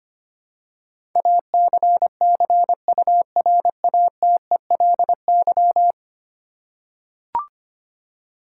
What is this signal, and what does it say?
Morse code, heard as a keyed tone, about 25 wpm, 700 Hz. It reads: ACCURATELY
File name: Ten Letter Words - 25wpm